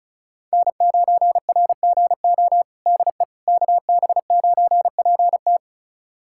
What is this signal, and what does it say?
Morse code, heard as a keyed tone, about 36 wpm, 700 Hz. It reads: N9RGO DE KB9PT
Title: USA Callsign DE Callsign - 5 Characters - 35wpm